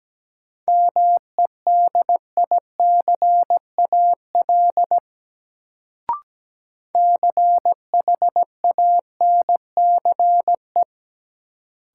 Morse code, heard as a keyed tone, about 17 wpm, 700 Hz. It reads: MEDICAL CHANCE
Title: Sentences from Top 800 Words - New Word Drill - 17wpm